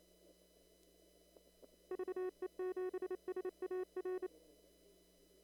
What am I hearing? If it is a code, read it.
Morse code, heard as a keyed tone, about 27 wpm, 370 Hz. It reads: VE7SAR
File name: CW / kt5j wpx cw-op-k5pi